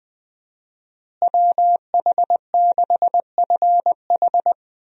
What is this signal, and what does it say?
Morse code, heard as a keyed tone, about 20 wpm, 700 Hz. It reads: WH6FH
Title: USA Call Signs - 5 Characters - Repeated 3x - 20wpm